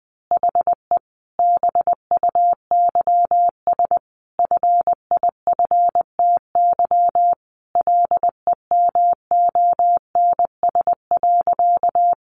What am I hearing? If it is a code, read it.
Morse code, heard as a keyed tone, about 19 wpm, 700 Hz. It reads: HE BUYS FIFTY LEMONS.
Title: Additional Easy Phrases 20wpm